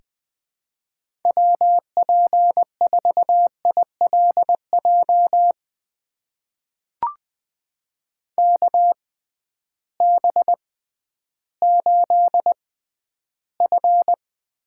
Morse code, heard as a keyed tone, about 20 wpm, spaced as for 12 wpm, 700 Hz. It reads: WP4ILJ KB8F
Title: USA Call Signs - 6 Characters - 6x Character Spacing - 20wpm